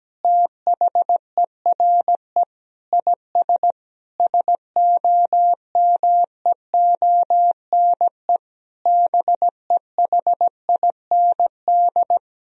Morse code, heard as a keyed tone, about 17 wpm, 700 Hz. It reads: THERE IS SOMEONE BEHIND
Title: Sentences from Top 1000 Words - Review Part 4 - 17wpm